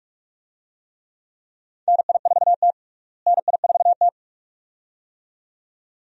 Morse code, heard as a keyed tone, about 47 wpm, 700 Hz. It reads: NI4T NI4T
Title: USA Call Signs - 4 Characters - Repeated 2x - 45wpm